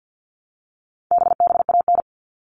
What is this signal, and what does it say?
Morse code, heard as a keyed tone, about 50 wpm, 700 Hz. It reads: BBAN
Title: All Cut Numbers - 4-Digit 50wpm